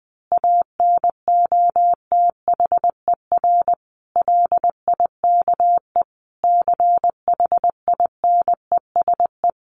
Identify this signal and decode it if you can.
Morse code, heard as a keyed tone, about 20 wpm, 700 Hz. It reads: ANOTHER LIKE CHINESE
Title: A Sound Of Thunder 20wpm